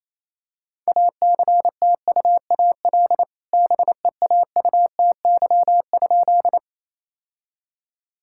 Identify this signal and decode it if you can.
Morse code, heard as a keyed tone, about 28 wpm, 700 Hz. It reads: ACTUAL BEAUTY?
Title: Sentences from Top 1000 Words - Review Part 5 - 28wpm